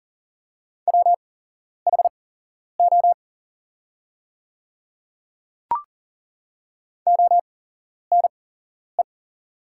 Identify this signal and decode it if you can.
Morse code, heard as a keyed tone, about 40 wpm, spaced as for 11 wpm, 700 Hz. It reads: WHO ONE
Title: 8x Character Spacing - Top 100 Words - 40wpm